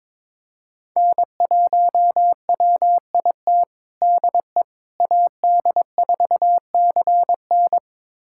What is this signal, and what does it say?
Morse code, heard as a keyed tone, about 22 wpm, 700 Hz. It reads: N1WIT DE AD4CN
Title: USA Callsign DE Callsign - 5 Characters - 22wpm